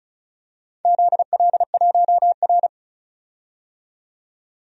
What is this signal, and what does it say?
Morse code, heard as a keyed tone, about 34 wpm, 700 Hz. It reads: ZL1R
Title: DXCC-100 Call Signs - 4 Characters - 35wpm